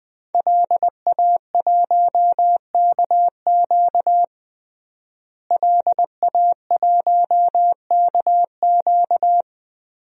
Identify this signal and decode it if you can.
Morse code, heard as a keyed tone, about 20 wpm, 700 Hz. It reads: LA1KQ LA1KQ
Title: DXCC-100 Call Signs - 5 Characters - Repeated 3x - 20wpm